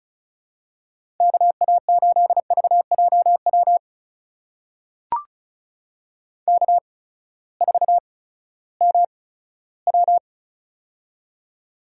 Morse code, heard as a keyed tone, about 36 wpm, spaced as for 18 wpm, 700 Hz. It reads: KA8VJW K4MW
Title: USA Call Signs - 8x Character Spacing - 35wpm